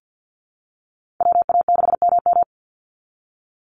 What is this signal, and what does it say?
Morse code, heard as a keyed tone, about 50 wpm, 700 Hz. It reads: WA6MM
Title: NA SOTA Activator Callsigns 50wpm